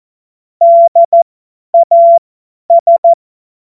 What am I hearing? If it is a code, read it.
Morse code, heard as a keyed tone, about 14 wpm, spaced as for 10 wpm, 680 Hz. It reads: DAS
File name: Morsekurs / Witze / Witz 21. März 2023 50 F70